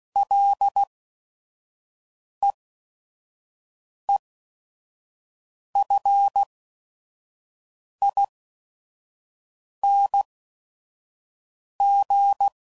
Morse code, heard as a keyed tone, about 16 wpm, spaced as for 5 wpm, 800 Hz. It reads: LEEFING
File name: Prefixes / FLs & THs / Single Fs Ls